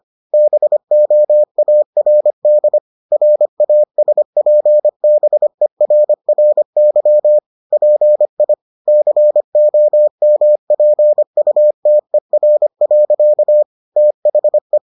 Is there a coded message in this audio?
Yes: Morse code from 0.3 to 14.8 s, heard as a keyed tone, about 25 wpm, 600 Hz. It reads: BOARD RASPBERRY PI COMPUTER. THE